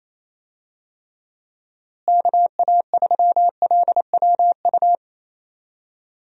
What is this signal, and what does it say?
Morse code, heard as a keyed tone, about 28 wpm, 700 Hz. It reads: KA3LWU
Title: USA Call Signs - 6 Characters - Repeated 2x - 28wpm